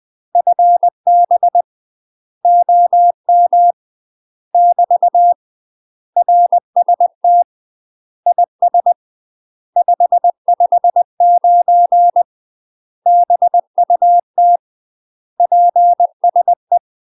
Morse code, heard as a keyed tone, about 20 wpm, 700 Hz. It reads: FB OM <BT> RST IS 559 BUT PSE
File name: UK MASTER QSOs TRAINING/20WPM